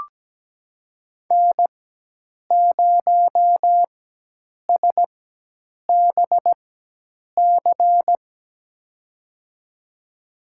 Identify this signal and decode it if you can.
Morse code, heard as a keyed tone, about 17 wpm, spaced as for 8 wpm, 700 Hz. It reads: N0SBC
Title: USA Call Signs - 5 Characters - 4x Character Spacing - 17wpm